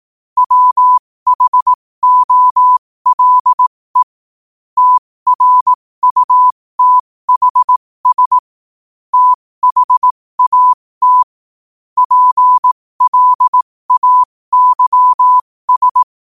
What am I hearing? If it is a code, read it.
Morse code, heard as a keyed tone, about 18 wpm, spaced as for 15 wpm, 1000 Hz. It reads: WHOLE TRUTHS THAT PLAYS
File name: qotdfiles / 15wpm